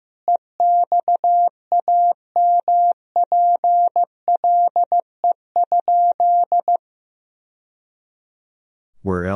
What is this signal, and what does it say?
Morse code, heard as a keyed tone, about 15 wpm, 700 Hz. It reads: EXAMPLE?
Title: Sentences from Top 1000 Words - Review Part 1 - 15wpm